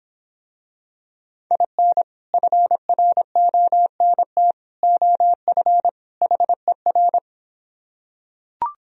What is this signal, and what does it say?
Morse code, heard as a keyed tone, about 26 wpm, 700 Hz. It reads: IN FRONT OF HER